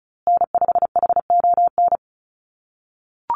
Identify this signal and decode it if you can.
Morse code, heard as a keyed tone, about 34 wpm, 700 Hz. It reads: N5HON
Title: USA Call Signs Repeated 2x 35wpm